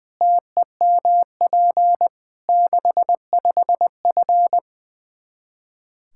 Morse code, heard as a keyed tone, about 21 wpm, 700 Hz. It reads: TEMP 65F
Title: Practice QSO - Exchange 2 - Info - 20wpm